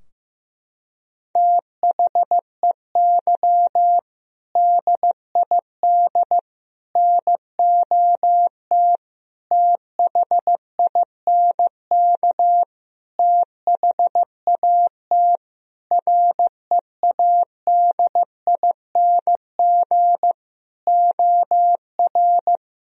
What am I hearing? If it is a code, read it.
Morse code, heard as a keyed tone, about 15 wpm, 700 Hz. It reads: THEY DID NOT THINK THAT READING OR
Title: Sentences from Top 1000 Words - Review Part 3 - 15wpm